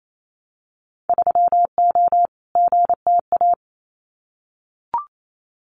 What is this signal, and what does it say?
Morse code, heard as a keyed tone, about 28 wpm, 700 Hz. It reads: 3O GTA
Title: Winter Field Day Exchanges Encore 28wpm